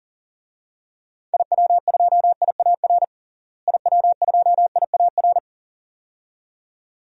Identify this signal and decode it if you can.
Morse code, heard as a keyed tone, about 40 wpm, 700 Hz. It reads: IW2IAR IW2IAR
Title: DXCC-100 Call Signs - 6 Characters - Repeated 2x - 40wpm